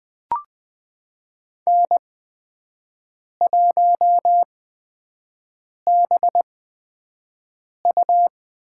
Morse code, heard as a keyed tone, about 20 wpm, spaced as for 5 wpm, 700 Hz. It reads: N1BU
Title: USA Call Signs - 8x Character Spacing - 20wpm